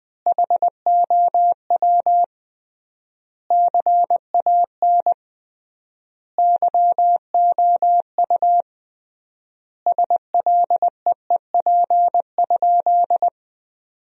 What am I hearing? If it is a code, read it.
Morse code, heard as a keyed tone, about 20 wpm, 700 Hz. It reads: HOW CAN YOU SLEEP?
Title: 3x Word-Spacing - Sentences from Top 500 Words - 20wpm